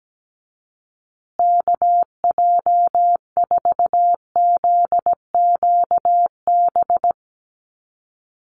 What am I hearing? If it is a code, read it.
Morse code, heard as a keyed tone, about 17 wpm, 700 Hz. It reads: KJ4ZQB
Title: USA Call Signs Repeated 2x 17wpm